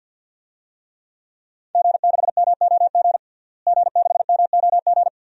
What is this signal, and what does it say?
Morse code, heard as a keyed tone, about 47 wpm, 700 Hz. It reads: G6MOG G6MOG
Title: DXCC-100 Call Signs - 5 Characters - Repeated 3x - 50wpm